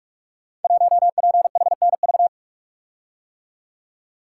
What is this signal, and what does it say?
Morse code, heard as a keyed tone, about 44 wpm, 700 Hz. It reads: 1PHNV
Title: Code Groups - 5 Letter:Numbers - 45wpm